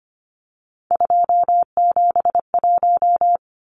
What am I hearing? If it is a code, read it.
Morse code, heard as a keyed tone, about 25 wpm, 690 Hz. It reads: 271
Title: Numbers - 3-Digit 25wpm